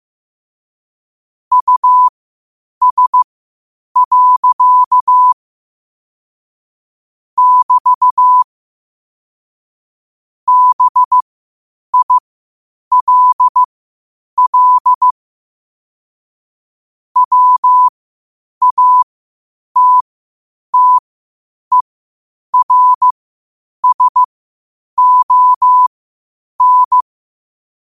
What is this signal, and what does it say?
Morse code, heard as a keyed tone, about 15 wpm, spaced as for 8 wpm, 1000 Hz. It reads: US. <BT> BILL WATTERSON
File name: qotdfiles / 8wpm